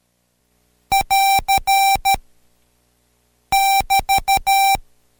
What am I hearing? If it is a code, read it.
Morse code, heard as a keyed tone, about 13 wpm, 780 Hz. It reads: <AR> <BT>
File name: Code Quick / AudioDownld2 / Seg127-Code Quick Prosigns 2